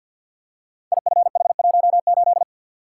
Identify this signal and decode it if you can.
Morse code, heard as a keyed tone, about 51 wpm, 700 Hz. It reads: IFH18
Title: Code Groups - 5 Letter:Numbers - 50wpm